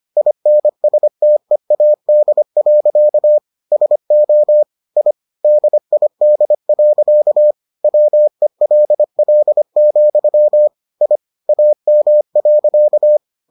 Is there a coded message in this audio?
Yes: Morse code from 0.2 to 13.2 s, heard as a keyed tone, about 25 wpm, 600 Hz. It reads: INSTEAD. SO I DID. WELL, I AM.